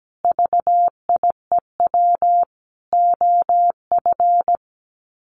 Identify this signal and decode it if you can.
Morse code, heard as a keyed tone, about 17 wpm, 700 Hz. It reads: VIEW OF
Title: Sets of 5 Words 17wpm